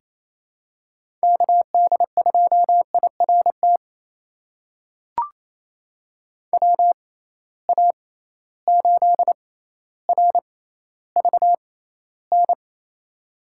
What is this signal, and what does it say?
Morse code, heard as a keyed tone, about 28 wpm, spaced as for 14 wpm, 700 Hz. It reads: KD2IRT WA8RVN
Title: USA Call Signs - 6 Characters - 6x Character Spacing - 28wpm